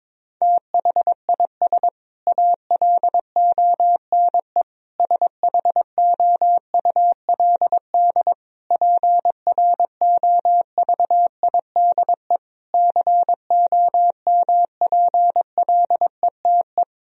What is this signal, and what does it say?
Morse code, heard as a keyed tone, about 22 wpm, 700 Hz. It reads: THIS ALONE SHOULD PROVIDE COMPLETE
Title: Sentences from Top 1000 Words - Review Part 4 - 22wpm